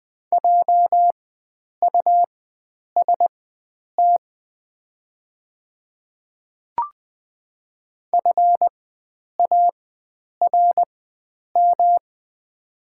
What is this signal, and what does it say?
Morse code, heard as a keyed tone, about 20 wpm, spaced as for 9 wpm, 700 Hz. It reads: JUST FARM
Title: 4x Character Spacing - Top 500 Words - 20wpm